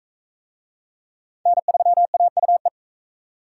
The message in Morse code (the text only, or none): N3AUE